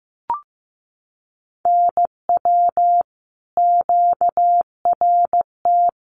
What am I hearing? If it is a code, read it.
Morse code, heard as a keyed tone, about 15 wpm, 700 Hz. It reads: NW QRT
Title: Common QSO Phrases 15wpm